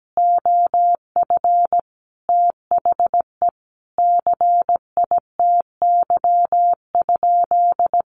Morse code, heard as a keyed tone, about 17 wpm, 700 Hz. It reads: OF THE CITY?
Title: Sentences from Top 1300 Words 17wpm